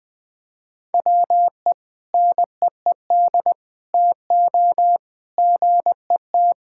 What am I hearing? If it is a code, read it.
Morse code, heard as a keyed tone, about 20 wpm, 700 Hz. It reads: WE NEED TO GET
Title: Sentences with 6-Word Phrases - 20wpm